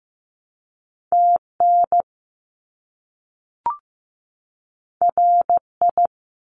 Morse code, heard as a keyed tone, about 15 wpm, 700 Hz. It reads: TN RI
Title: US States 15wpm